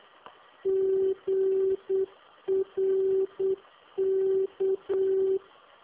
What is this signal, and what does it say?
Morse code, heard as a keyed tone, about 8 wpm, 380 Hz. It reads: GRK